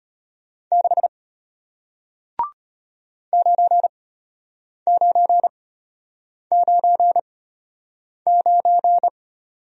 Morse code, heard as a keyed tone, about 33 wpm, spaced as for 8 wpm, 700 Hz. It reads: 6 9999